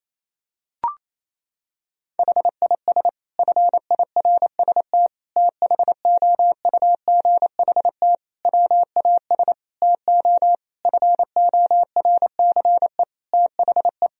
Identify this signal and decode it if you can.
Morse code, heard as a keyed tone, about 28 wpm, 700 Hz. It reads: HIS FIRST THOUGHT WAS TO FORCE THE
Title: Sentences from Top 1000 Words - Review Part 3 - 28wpm